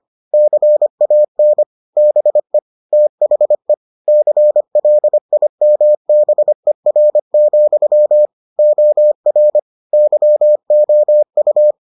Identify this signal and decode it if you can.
Morse code, heard as a keyed tone, about 25 wpm, 600 Hz. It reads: CAN BE THE CLIMBER, OR YOU